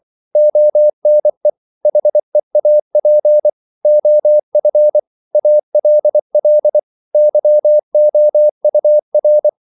Morse code, heard as a keyed tone, about 24 wpm, 600 Hz. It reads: ONE HEAP OF ALL YOUR
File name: CW / Practice / If24